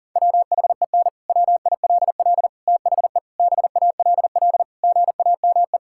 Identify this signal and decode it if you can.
Morse code, heard as a keyed tone, about 40 wpm, 700 Hz. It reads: WHEN WILL THE BALL GAME
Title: Sentences from Top 1000 Words - Review Part 6 - 40wpm